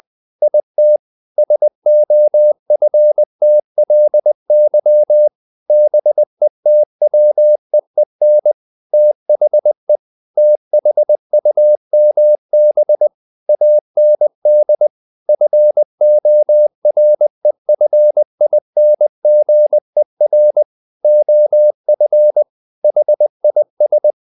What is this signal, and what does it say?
Morse code, heard as a keyed tone, about 20 wpm, 600 Hz. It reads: IT SOFTLY BETWEEN THE THUMB AND FOREFINGER OF HIS
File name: the old man and the sea / the old man and the sea part 1 20wpm